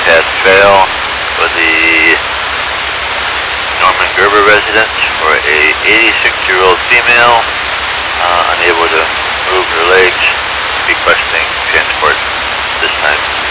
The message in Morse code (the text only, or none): none